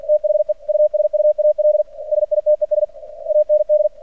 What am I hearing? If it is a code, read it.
Morse code, heard as a keyed tone, about 49 wpm, 600 Hz. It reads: THE USUAL SITES AND
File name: archived / CRSnet